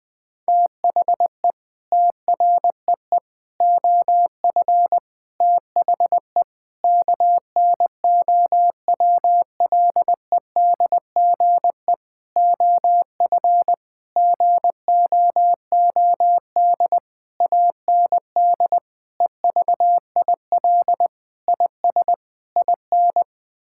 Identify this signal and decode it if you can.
Morse code, heard as a keyed tone, about 20 wpm, 700 Hz. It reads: THE TREE OF THE KNOWLEDGE OF GOOD AND EVIL IS IN
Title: Sentences with 9-Word Phrases - 20wpm